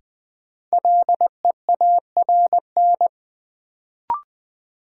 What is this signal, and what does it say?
Morse code, heard as a keyed tone, about 20 wpm, 700 Hz. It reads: LEARN
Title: Sentences from Top 300 Words - New Word Drill - 20wpm